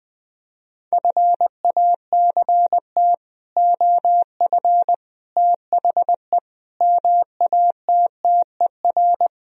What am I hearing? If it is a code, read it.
Morse code, heard as a keyed tone, about 20 wpm, 700 Hz. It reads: FACT OF THE MATTER